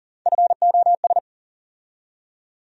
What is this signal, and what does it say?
Morse code, heard as a keyed tone, about 40 wpm, 700 Hz. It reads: FOS
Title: USA Call Sign Postfixes - 3 Letters 40wpm